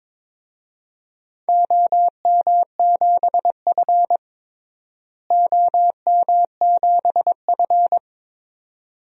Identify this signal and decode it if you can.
Morse code, heard as a keyed tone, about 22 wpm, 700 Hz. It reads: OM7F OM7F
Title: DXCC-100 Call Signs - 4 Characters - Repeated 2x - 22wpm